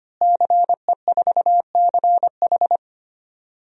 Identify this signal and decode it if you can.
Morse code, heard as a keyed tone, about 25 wpm, 700 Hz. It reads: CE4CH
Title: DXCC-100 Call Signs - 5 Characters - Repeated 2x - 25wpm